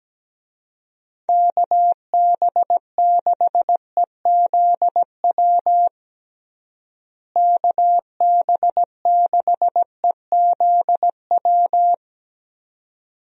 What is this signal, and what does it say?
Morse code, heard as a keyed tone, about 17 wpm, 700 Hz. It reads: KB6EZW KB6EZW
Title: USA Call Signs - 6 Characters - Repeated 2x - 17wpm